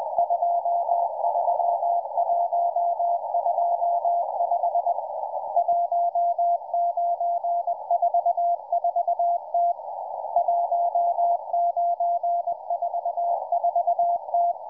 Morse code, heard as a keyed tone, about 20 wpm, 690 Hz. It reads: 23125 1944T 1944T